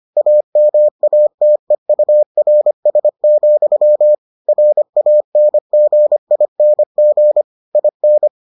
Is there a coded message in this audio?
Yes: Morse code from 0.2 to 8.3 s, heard as a keyed tone, about 25 wpm, 600 Hz. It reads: AMATEURS, RANGING IN